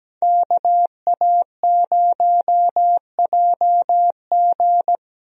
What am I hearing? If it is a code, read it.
Morse code, heard as a keyed tone, about 17 wpm, 700 Hz. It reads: KA0JG
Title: USA Call Signs - 5 Characters - Repeated 3x - 17wpm